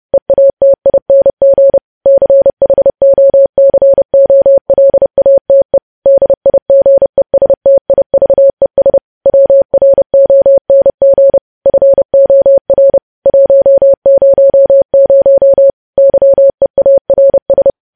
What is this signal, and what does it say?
Morse code, heard as a keyed tone, about 30 wpm, 550 Hz. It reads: EATING CHOCOLATE DIGESTIVES WRONG FOR 100 YEARS
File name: podcast / News headlines-1x-30x30